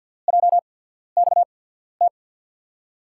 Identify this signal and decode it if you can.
Morse code, heard as a keyed tone, about 47 wpm, spaced as for 13 wpm, 700 Hz. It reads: JXT
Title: USA Call Signs - 6 Characters - 8x Character Spacing - 50wpm